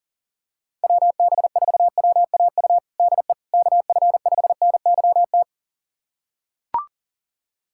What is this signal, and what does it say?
Morse code, heard as a keyed tone, about 40 wpm, 700 Hz. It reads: WB4WAU DE KF5NYT
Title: USA Callsign DE Callsign - 6 Characters - 40wpm